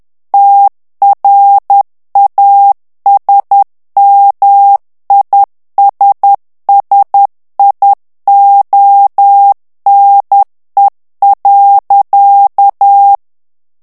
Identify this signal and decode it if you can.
Morse code, heard as a keyed tone, about 10 wpm, 800 Hz. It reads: TRASMISSIONE.
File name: morse mp3 / ITALIEN texte clair 1 vitesse 10 mots minute